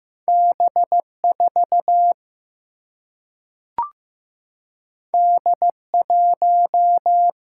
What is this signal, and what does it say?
Morse code, heard as a keyed tone, about 15 wpm, 700 Hz. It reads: B 4 D 1
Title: Single-Letter Single-Number 15wpm